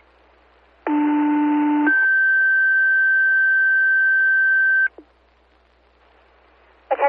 Two-tone page, heard as a one-second tone, then a three-second tone, about 300 then 1550 Hz. It is heard from 0.9 to 4.9 s.
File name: fire / rowe